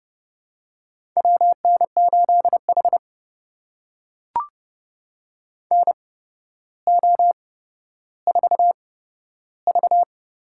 Morse code, heard as a keyed tone, about 31 wpm, spaced as for 13 wpm, 700 Hz. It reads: WN8H NO4V
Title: USA Call Signs - 4 Characters - 8x Character Spacing - 30wpm